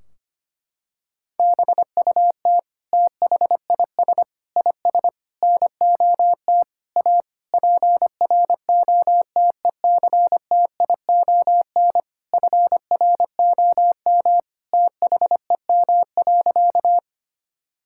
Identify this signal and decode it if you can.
Morse code, heard as a keyed tone, about 25 wpm, 700 Hz. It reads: BUT THIS IS NOT A PROTECTION FROM THEM.
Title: Sentences from Top 1600 Words 25wpm